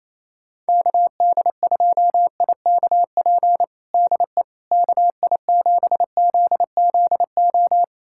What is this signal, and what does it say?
Morse code, heard as a keyed tone, about 28 wpm, 700 Hz. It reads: KD2IKP DE KI7ZZO
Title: USA Callsign DE Callsign - 6 Characters - 28wpm